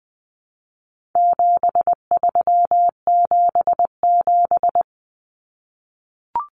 Morse code, heard as a keyed tone, about 20 wpm, 700 Hz. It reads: 7377
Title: Common Cut Numbers - 4-Digit 20wpm